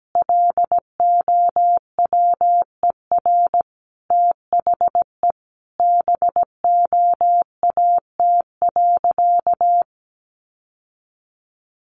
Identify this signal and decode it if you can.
Morse code, heard as a keyed tone, about 17 wpm, 700 Hz. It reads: LOWER THE BOAT.
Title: Sentences from Top 700 Words 17wpm